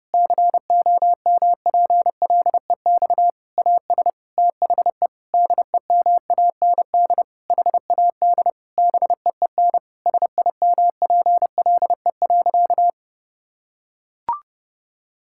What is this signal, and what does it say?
Morse code, heard as a keyed tone, about 32 wpm, 700 Hz. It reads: COMPLEX AS THE DEMAND HAD BEEN SIMPLE.
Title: Sentences from Top 1000 Words - Review Part 6 - 30wpm